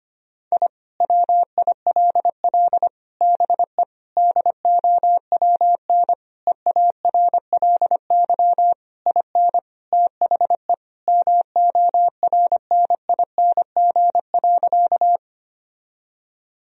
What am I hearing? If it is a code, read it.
Morse code, heard as a keyed tone, about 25 wpm, 700 Hz. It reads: I WILL BE DOWN EARLY IN THE MORNING.
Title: Sentences from Top 1000 Words - Review Part 2 - 25wpm